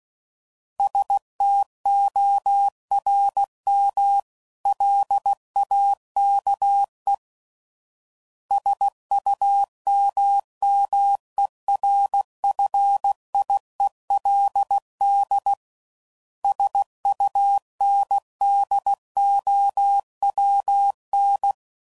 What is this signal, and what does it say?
Morse code, heard as a keyed tone, about 16 wpm, 800 Hz. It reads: STORM LAKE SUMMERFIELD SUNDOWN
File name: CW Code Groups @15WPM